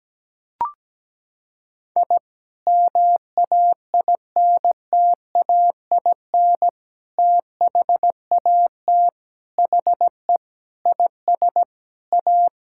Morse code, heard as a keyed tone, about 17 wpm, 700 Hz. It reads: I MAINTAIN THAT HE IS A
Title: Sentences from Top 1000 Words - Review Part 5 - 17wpm